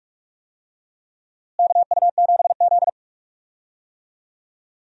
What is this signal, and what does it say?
Morse code, heard as a keyed tone, about 46 wpm, 700 Hz. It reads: KU7Z
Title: USA Call Signs - 4 Characters - Repeated 3x - 45wpm